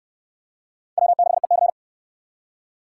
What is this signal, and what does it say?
Morse code, heard as a keyed tone, about 61 wpm, 700 Hz. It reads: W6EC